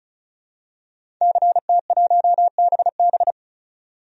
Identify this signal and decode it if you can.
Morse code, heard as a keyed tone, about 36 wpm, 700 Hz. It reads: CT1BB
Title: DXCC-100 Call Signs - 5 Characters - 35wpm